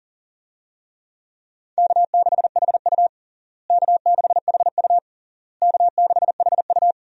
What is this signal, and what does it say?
Morse code, heard as a keyed tone, about 40 wpm, 700 Hz. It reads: K6HU K6HU K6HU
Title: USA Call Signs - 4 Characters - Repeated 3x - 40wpm